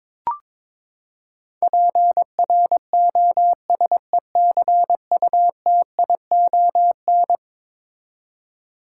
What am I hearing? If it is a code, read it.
Morse code, heard as a keyed tone, about 22 wpm, 700 Hz. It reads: PROSECUTION
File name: practice / 22wpm / Eleven Letter Words - 22wpm